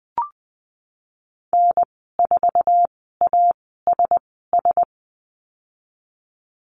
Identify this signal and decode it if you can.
Morse code, heard as a keyed tone, about 20 wpm, spaced as for 15 wpm, 700 Hz. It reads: N4ASS
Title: 048 Lesson 9g - Number 4 - Call Signs 20wpm